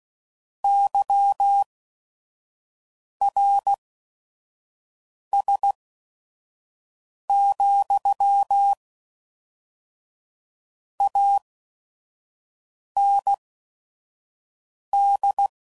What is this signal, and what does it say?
Morse code, heard as a keyed tone, about 16 wpm, spaced as for 5 wpm, 800 Hz. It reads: YRS, AND
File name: CW Code Groups @5WPM / QSOs / WD0FBT 1